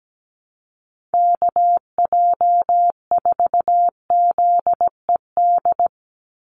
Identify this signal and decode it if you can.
Morse code, heard as a keyed tone, about 17 wpm, 700 Hz. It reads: KJ4ZED
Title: USA Call Signs 17wpm